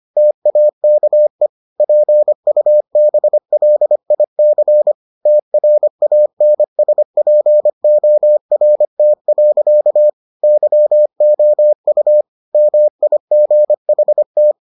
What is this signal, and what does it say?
Morse code, heard as a keyed tone, about 25 wpm, 600 Hz. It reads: TAKE PUBLIC TRANSPORT. YOU MIGHT